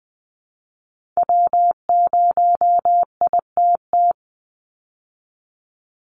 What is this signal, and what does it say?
Morse code, heard as a keyed tone, about 19 wpm, 700 Hz. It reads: W0ITT
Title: Call Signs Repeated 3x - 20wpm